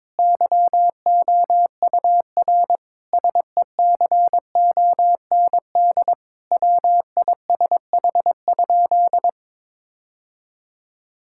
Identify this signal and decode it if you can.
Morse code, heard as a keyed tone, about 22 wpm, 700 Hz. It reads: YOUR SECOND WISH?
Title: Sentences from Top 1600 Words 22wpm